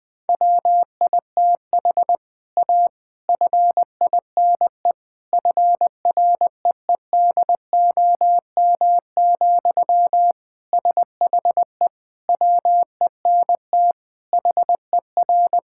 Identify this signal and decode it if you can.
Morse code, heard as a keyed tone, about 20 wpm, 700 Hz. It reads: WITH A FINE FREEDOM, SHE WENT HER
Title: Sentences from Top 1000 Words - Review Part 5 - 20wpm